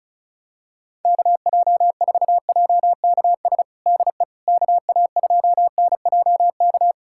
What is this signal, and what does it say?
Morse code, heard as a keyed tone, about 36 wpm, 700 Hz. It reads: KJ4JKS DE KA2NJK
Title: USA Callsign DE Callsign - 6 Characters - 35wpm